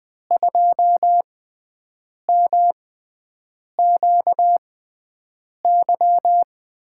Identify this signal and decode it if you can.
Morse code, heard as a keyed tone, about 20 wpm, spaced as for 7 wpm, 700 Hz. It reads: 2MQY
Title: USA Call Signs - 6x Character Spacing - 20wpm